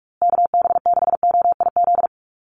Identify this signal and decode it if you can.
Morse code, heard as a keyed tone, about 44 wpm, 700 Hz. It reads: KB6OIZ
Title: USA Call Signs Repeated 2x 45wpm